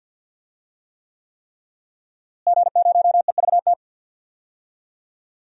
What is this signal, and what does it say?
Morse code, heard as a keyed tone, about 45 wpm, 700 Hz. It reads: G0EVT
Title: DXCC-100 Call Signs - 5 Characters - Repeated 3x - 50wpm